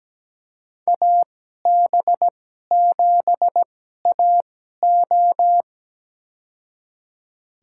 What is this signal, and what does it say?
Morse code, heard as a keyed tone, about 17 wpm, spaced as for 12 wpm, 700 Hz. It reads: AB7AO
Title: USA Call Signs - 5 Characters - 2x Character Spacing - 17wpm